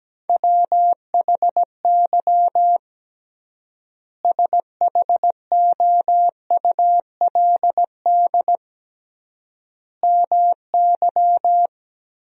Morse code, heard as a keyed tone, about 17 wpm, 700 Hz. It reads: WHY SHOULD MY
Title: 3x Word-Spacing - Sentences from Top 500 Words - 17wpm